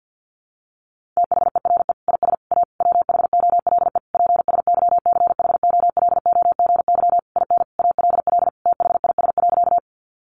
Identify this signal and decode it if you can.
Morse code, heard as a keyed tone, about 48 wpm, 700 Hz. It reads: THERE IS A WHOLE PSYCHOLOGY IN ALL THIS.
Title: Sentences from Top 1400 Words 50wpm